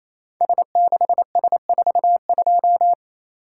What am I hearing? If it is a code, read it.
Morse code, heard as a keyed tone, about 27 wpm, 700 Hz. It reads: S6S42
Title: Code Groups - 5 Letter:Numbers - 28wpm